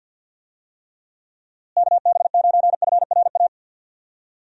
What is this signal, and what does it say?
Morse code, heard as a keyed tone, about 50 wpm, 700 Hz. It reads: KB9FRA
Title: USA Call Signs - 6 Characters - Repeated 3x - 50wpm